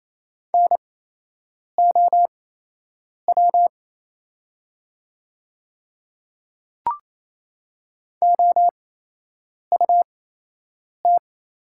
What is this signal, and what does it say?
Morse code, heard as a keyed tone, about 28 wpm, spaced as for 8 wpm, 700 Hz. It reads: NOW OUT
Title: 8x Character Spacing - Top 100 Words - 28wpm